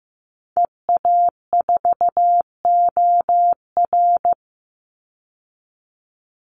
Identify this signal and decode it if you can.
Morse code, heard as a keyed tone, about 15 wpm, 700 Hz. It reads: EA4OR
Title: Call Signs Repeated 2x - 15wpm